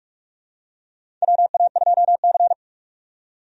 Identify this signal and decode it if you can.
Morse code, heard as a keyed tone, about 44 wpm, 700 Hz. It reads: WA2C
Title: USA Call Signs - 4 Characters - 45wpm